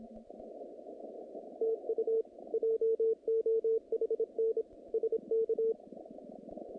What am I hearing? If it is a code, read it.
Morse code, heard as a keyed tone, about 27 wpm, 440 Hz. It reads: TU JOHN SK
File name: contests / naqp-cw-k5tr-op-k5pi- -2300